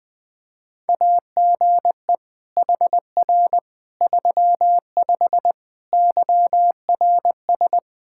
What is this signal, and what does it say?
Morse code, heard as a keyed tone, about 20 wpm, 700 Hz. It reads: AGE HR 35 YRS
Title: K1USN SST QSO Practice 20wpm